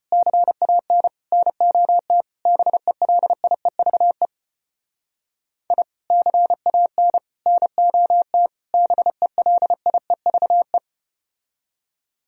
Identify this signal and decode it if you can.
Morse code, heard as a keyed tone, about 34 wpm, 700 Hz. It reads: CAN NOT BELIEVE I CAN NOT BELIEVE